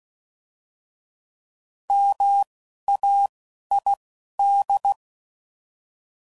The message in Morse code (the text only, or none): MAID